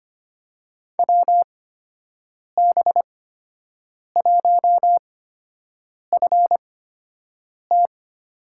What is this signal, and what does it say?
Morse code, heard as a keyed tone, about 25 wpm, spaced as for 7 wpm, 700 Hz. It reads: WB1FT
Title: USA Call Signs - 8x Character Spacing - 25wpm